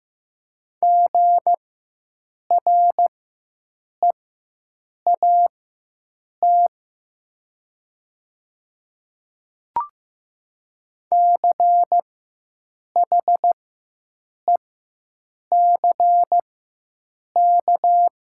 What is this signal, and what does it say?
Morse code, heard as a keyed tone, about 15 wpm, spaced as for 7 wpm, 700 Hz. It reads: GREAT CHECK
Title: 4x Character Spacing - Top 500 Words - 15wpm